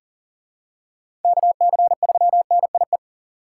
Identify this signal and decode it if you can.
Morse code, heard as a keyed tone, about 40 wpm, 700 Hz. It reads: KC3NIE
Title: K1USN SST QSO Practice - 40wpm